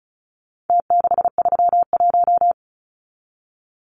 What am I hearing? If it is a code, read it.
Morse code, heard as a keyed tone, about 34 wpm, 700 Hz. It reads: T631
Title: Numbers - Contest Style - 4-Digit 35wpm